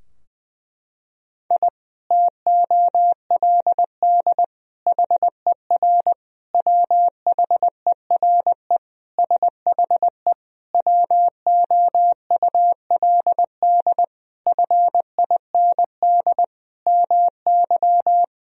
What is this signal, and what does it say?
Morse code, heard as a keyed tone, about 21 wpm, 700 Hz. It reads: I TOLD HER WHERE SHE WOULD FIND MY
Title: Sentences from Top 1000 Words - Review Part 2 - 20wpm